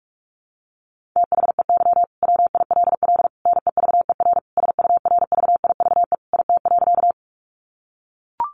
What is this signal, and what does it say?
Morse code, heard as a keyed tone, about 44 wpm, 700 Hz. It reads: THEY WILL NEVER SURVIVE IT.
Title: Sentences from Top 1300 Words 45wpm